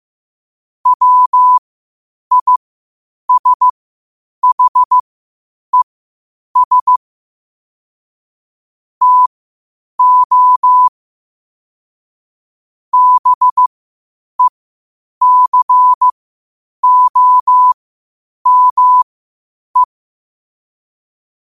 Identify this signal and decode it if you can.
Morse code, heard as a keyed tone, about 15 wpm, spaced as for 8 wpm, 1000 Hz. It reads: WISHES TO BECOME
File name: qotdfiles / 8wpm